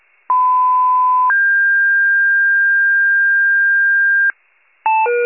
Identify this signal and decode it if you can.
Two-tone page, heard as a one-second tone, then a three-second tone, about 1010 then 1600 Hz. It is heard from 0.3 to 4.3 s.
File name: pager / 94 RESCUE DUTY / wcalert